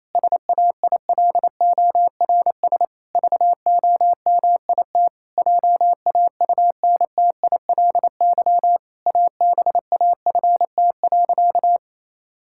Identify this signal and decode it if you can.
Morse code, heard as a keyed tone, about 27 wpm, 700 Hz. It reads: SAILORS VOMIT JAUNTILY ABAFT.
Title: Pangram Challenge v2 - 28wpm